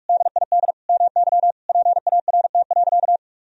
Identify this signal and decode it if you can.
Morse code, heard as a keyed tone, about 47 wpm, 700 Hz. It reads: DID MY PART.
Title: Sentences from Top 1000 Words - Review Part 1 - 45wpm